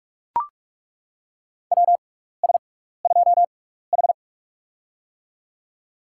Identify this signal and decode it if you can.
Morse code, heard as a keyed tone, about 43 wpm, spaced as for 15 wpm, 700 Hz. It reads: WS2H
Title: USA Call Signs - 4 Characters - 6x Character Spacing - 45wpm